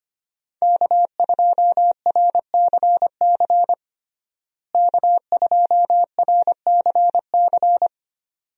Morse code, heard as a keyed tone, about 25 wpm, 700 Hz. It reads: K2RCC K2RCC
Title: USA Call Signs - 5 Characters - Repeated 2x - 25wpm